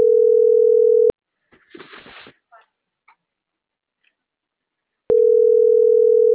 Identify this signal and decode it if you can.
Ringback tone (North America), 440 + 480 Hz.
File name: CENTRAL MAERSK / out luis.herrera - .30525